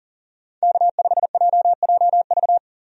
Morse code, heard as a keyed tone, about 40 wpm, 700 Hz. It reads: K5JJV
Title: USA Call Signs - 5 Characters - Repeated 2x - 40wpm